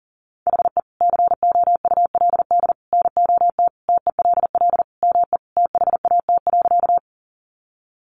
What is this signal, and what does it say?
Morse code, heard as a keyed tone, about 40 wpm, 700 Hz. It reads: HE COULD NOT TELL ME THAT.
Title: Sentences from Top 600 Words 40wpm